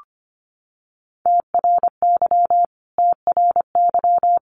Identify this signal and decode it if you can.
Morse code, heard as a keyed tone, about 25 wpm, 700 Hz. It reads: TRY TRY
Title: Top 500 Words Repeated 25wpm